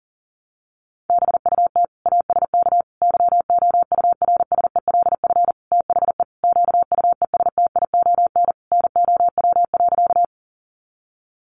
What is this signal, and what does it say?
Morse code, heard as a keyed tone, about 37 wpm, 700 Hz. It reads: BUT ASK YOURSELF THE QUESTION NOW.
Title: Sentences from Top 600 Words 40wpm